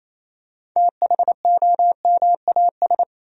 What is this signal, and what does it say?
Morse code, heard as a keyed tone, about 28 wpm, 700 Hz. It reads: THOMAS
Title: USA Common Names - 28wpm